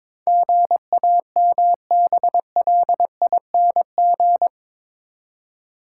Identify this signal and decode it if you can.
Morse code, heard as a keyed tone, about 22 wpm, 700 Hz. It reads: GAMBLING
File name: practice / 22wpm / Eight Letter Words - 22wpm